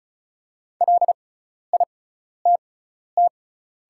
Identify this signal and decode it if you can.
Morse code, heard as a keyed tone, about 36 wpm, spaced as for 12 wpm, 690 Hz. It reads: LITT
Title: 6x Character Spacing - Top 200 Words - 35wpm